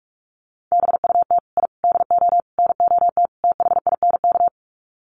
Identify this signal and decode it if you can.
Morse code, heard as a keyed tone, about 44 wpm, 700 Hz. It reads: BUT I DO NOT THINK
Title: Sets of 5 Words 45wpm